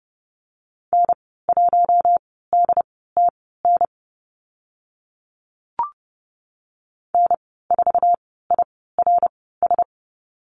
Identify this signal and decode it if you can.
Morse code, heard as a keyed tone, about 30 wpm, spaced as for 17 wpm, 700 Hz. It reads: N1DTN N4IRS
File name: Lesson 14g - Number 2 - Call Signs 30wpm